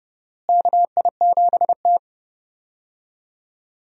Morse code, heard as a keyed tone, about 29 wpm, 700 Hz. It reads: KI7T
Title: USA Call Signs - 4 Characters - Repeated 3x - 30wpm